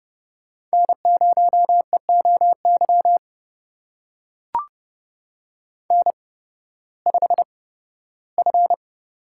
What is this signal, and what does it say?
Morse code, heard as a keyed tone, about 32 wpm, spaced as for 16 wpm, 700 Hz. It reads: N0EOY N5F
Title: USA Call Signs - 5 Characters - 8x Character Spacing - 30wpm